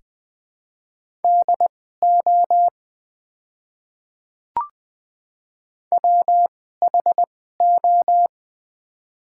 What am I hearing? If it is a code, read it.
Morse code, heard as a keyed tone, about 20 wpm, spaced as for 14 wpm, 700 Hz. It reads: DO WHO
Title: 2x Character Spacing - Top 100 Words - 20wpm